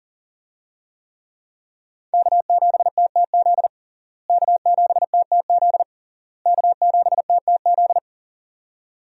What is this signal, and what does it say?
Morse code, heard as a keyed tone, about 40 wpm, 700 Hz. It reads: K7TTZ K7TTZ K7TTZ
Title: USA Call Signs - 5 Characters - Repeated 3x - 40wpm